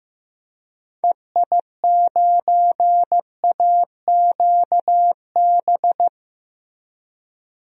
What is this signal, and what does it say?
Morse code, heard as a keyed tone, about 15 wpm, 700 Hz. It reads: EI9AQB
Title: DXCC-100 Call Signs - 6 Characters - Repeated 2x - 15wpm